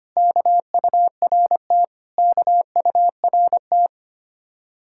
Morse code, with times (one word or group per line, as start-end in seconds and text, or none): 0.2-1.8 KURT
2.2-3.9 KURT